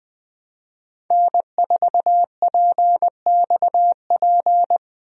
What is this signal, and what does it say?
Morse code, heard as a keyed tone, about 20 wpm, 700 Hz. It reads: N4PXP
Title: USA Call Signs - 5 Characters - Repeated 3x - 20wpm